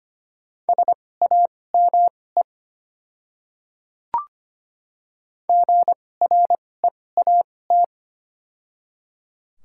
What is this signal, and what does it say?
Morse code, heard as a keyed tone, about 25 wpm, spaced as for 18 wpm, 700 Hz. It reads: SAME GREAT
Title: 2x Character Spacing - Top 300 Words - 25wpm